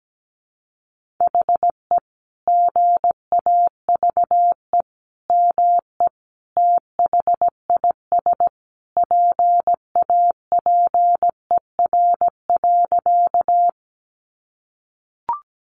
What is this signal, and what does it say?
Morse code, heard as a keyed tone, about 17 wpm, 700 Hz. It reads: HE GAVE ME THIS PAPER.
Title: Sentences from Top 500 Words 17wpm